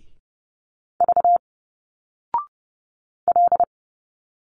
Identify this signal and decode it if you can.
Morse code, heard as a keyed tone, about 30 wpm, 700 Hz. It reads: V L
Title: Single Letters 30wpm